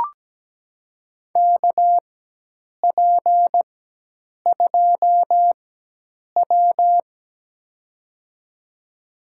Morse code, heard as a keyed tone, about 17 wpm, spaced as for 8 wpm, 700 Hz. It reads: KP2W
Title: USA Call Signs - 4x Character Spacing - 17wpm